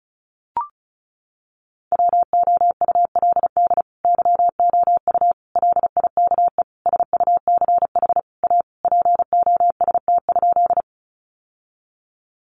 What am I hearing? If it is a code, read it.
Morse code, heard as a keyed tone, about 34 wpm, 700 Hz. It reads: WOULD YOU LIKE SUCH A POST?
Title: Sentences from Top 1000 Words 35wpm